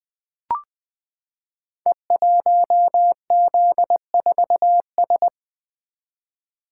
Morse code, heard as a keyed tone, about 20 wpm, 700 Hz. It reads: E1Z4S
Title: Code Groups - 5 Letter:Numbers - 20wpm